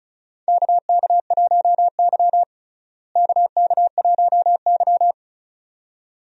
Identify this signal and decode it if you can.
Morse code, heard as a keyed tone, about 36 wpm, 700 Hz. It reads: KK1Y KK1Y
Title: USA Call Signs - 4 Characters - Repeated 2x - 35wpm